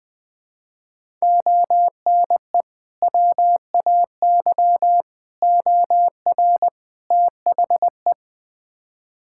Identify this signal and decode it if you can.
Morse code, heard as a keyed tone, about 20 wpm, 700 Hz. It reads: ONE WAY OR THE
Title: Build A Phrase - 5-Word Phrases - 20wpm